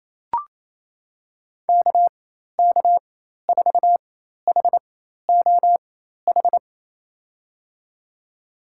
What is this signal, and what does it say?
Morse code, heard as a keyed tone, about 28 wpm, spaced as for 13 wpm, 700 Hz. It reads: KK4HOH
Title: USA Call Signs - 6 Characters - 4x Character Spacing - 28wpm